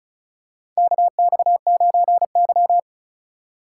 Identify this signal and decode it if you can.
Morse code, heard as a keyed tone, about 36 wpm, 700 Hz. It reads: KX9Y
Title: USA Call Signs - 4 Characters - 2x Character Spacing - 35wpm